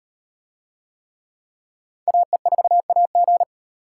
Morse code, heard as a keyed tone, about 36 wpm, 700 Hz. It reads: AE4AG